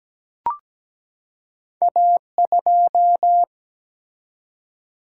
Morse code, heard as a keyed tone, about 17 wpm, 700 Hz. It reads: A2
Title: Single-Letter Single-Number 17wpm